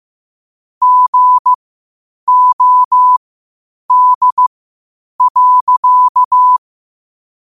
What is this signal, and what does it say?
Morse code, heard as a keyed tone, about 15 wpm, spaced as for 8 wpm, 1000 Hz. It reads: GOD.
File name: 8wpm